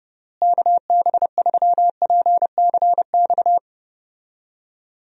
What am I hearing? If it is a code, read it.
Morse code, heard as a keyed tone, about 30 wpm, 700 Hz. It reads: KB3PCX
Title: USA Call Signs - 6 Characters - Repeated 3x - 30wpm